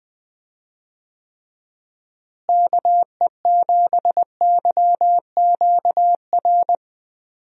Morse code, heard as a keyed tone, about 21 wpm, 700 Hz. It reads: KE7YQR
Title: USA Call Signs - 6 Characters - Repeated 2x - 20wpm